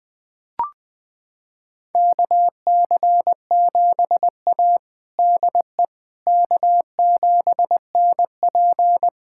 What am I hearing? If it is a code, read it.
Morse code, heard as a keyed tone, about 20 wpm, 700 Hz. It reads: KC7A DE K7NP
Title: USA Callsign DE Callsign - 4 Characters - 20wpm